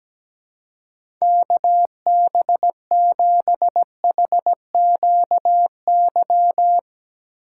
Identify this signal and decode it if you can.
Morse code, heard as a keyed tone, about 17 wpm, 700 Hz. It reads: KB7HQY
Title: USA Call Signs - 6x Character Spacing - 17wpm